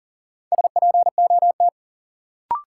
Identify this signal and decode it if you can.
Morse code, heard as a keyed tone, about 40 wpm, 700 Hz. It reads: SPOT
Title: Sentences from Top 1100 Words - New Word Drill Alt - 40wpm